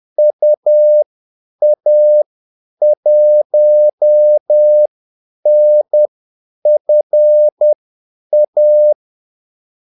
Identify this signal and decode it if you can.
Morse code, heard as a keyed tone, about 10 wpm, spaced as for 8 wpm, 600 Hz. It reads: UA1NFA
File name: Callsigns-10-8wpm